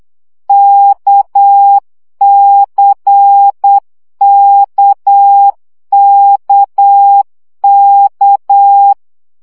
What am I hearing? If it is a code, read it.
Morse code, heard as a keyed tone, about 8 wpm, 800 Hz. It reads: KCKKK
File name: CW gyakorlas k c y 7 baud